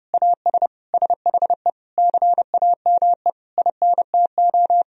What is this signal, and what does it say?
Morse code, heard as a keyed tone, about 32 wpm, 700 Hz. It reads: AS SHE CAME INTO
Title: Sentences from Top 1000 Words - Review Part 2 - 30wpm